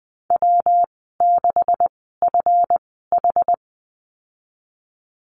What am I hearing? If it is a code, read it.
Morse code, heard as a keyed tone, about 20 wpm, spaced as for 14 wpm, 700 Hz. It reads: W6FH
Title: 161 Lesson 26g - Letter g - Call Signs 20wpm